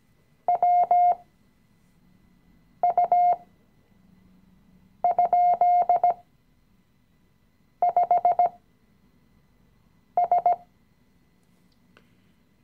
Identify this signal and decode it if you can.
Morse code, heard as a keyed tone, about 17 wpm, spaced as for 5 wpm, 710 Hz. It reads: WU?5S
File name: CW-Online-Kurs / Anfänger / Lektion 018